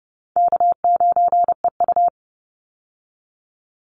Morse code, heard as a keyed tone, about 29 wpm, 700 Hz. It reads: K9EU
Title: Easy Call Signs Repeated 2x - 30wpm